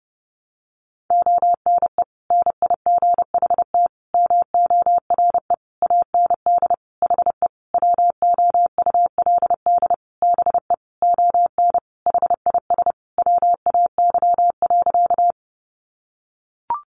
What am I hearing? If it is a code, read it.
Morse code, heard as a keyed tone, about 30 wpm, 700 Hz. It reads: ONE NIGHT MORE AND HE WOULD BE ON HIS WAY.
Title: Sentences from Top 300 Words 30wpm